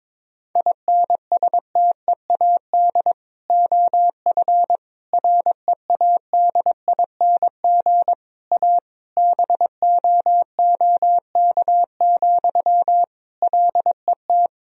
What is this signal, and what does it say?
Morse code, heard as a keyed tone, about 22 wpm, 700 Hz. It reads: INSTEAD OF READING A BOOK, LET
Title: Sentences from Top 1000 Words - Review Part 4 - 22wpm